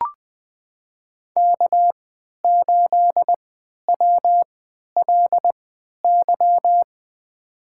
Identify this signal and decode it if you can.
Morse code, heard as a keyed tone, about 20 wpm, spaced as for 11 wpm, 700 Hz. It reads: K8WLY
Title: USA Call Signs - 3x Character Spacing - 20wpm